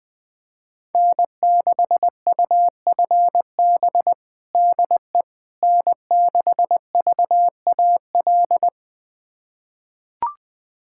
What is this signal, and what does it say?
Morse code, heard as a keyed tone, about 20 wpm, 700 Hz. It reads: N6UFB DE N6VAL
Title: USA Callsign DE Callsign - 5 Characters - 20wpm